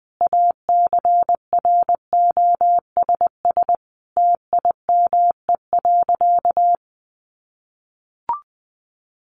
Morse code, heard as a keyed tone, about 20 wpm, 700 Hz. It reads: ACROSS TIME.
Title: A Sound Of Thunder 20wpm